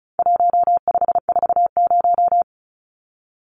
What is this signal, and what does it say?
Morse code, heard as a keyed tone, about 34 wpm, 700 Hz. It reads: 1540
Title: Numbers - 4-Digit 35wpm